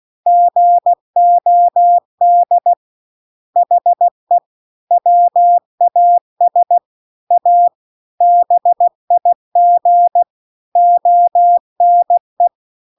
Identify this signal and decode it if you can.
Morse code, heard as a keyed tone, about 16 wpm, 700 Hz. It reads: GOD HE WAS A BIG ONE
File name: CW-The-Big-Twoheaded-River-2-16WPM